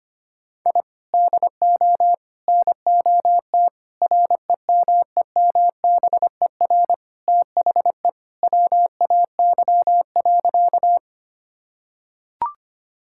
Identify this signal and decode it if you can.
Morse code, heard as a keyed tone, about 25 wpm, 700 Hz. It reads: I DO NOT REMEMBER THE WAY.
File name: practice / 25wpm / Sentences with 4-Word Phrases - 25wpm